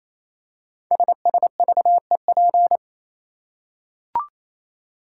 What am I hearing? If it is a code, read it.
Morse code, heard as a keyed tone, about 27 wpm, 700 Hz. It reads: SSVEP
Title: Code Groups - 5 Letters - 28wpm